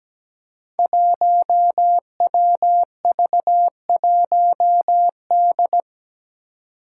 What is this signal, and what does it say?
Morse code, heard as a keyed tone, about 17 wpm, 700 Hz. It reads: 1WV1D
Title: Five Challenging Letter-Number Combinations 17wpm